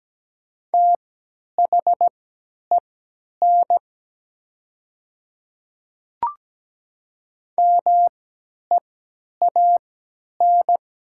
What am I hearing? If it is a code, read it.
Morse code, heard as a keyed tone, about 17 wpm, spaced as for 10 wpm, 700 Hz. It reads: THEN MEAN
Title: 3x Character Spacing - Top 300 Words - 17wpm